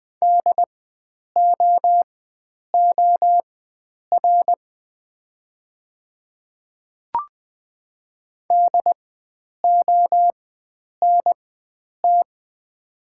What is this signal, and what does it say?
Morse code, heard as a keyed tone, about 20 wpm, spaced as for 9 wpm, 690 Hz. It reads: DOOR DONT
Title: 4x Character Spacing - Top 300 Words - 20wpm